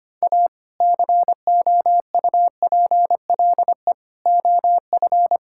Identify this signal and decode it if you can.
Morse code, heard as a keyed tone, about 25 wpm, 700 Hz. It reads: A COUPLE OF
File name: practice / 25wpm / Sentences with 2-Word Phrases - 25wpm